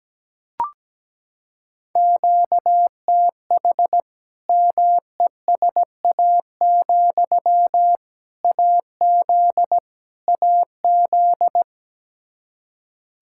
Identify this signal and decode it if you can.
Morse code, heard as a keyed tone, about 17 wpm, 700 Hz. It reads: QTH MESA, AZ AZ
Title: Practice QSO - Exchange 1 - Beginning - 17wpm